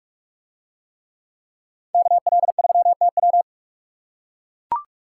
Morse code, heard as a keyed tone, about 44 wpm, 700 Hz. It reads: KL3TW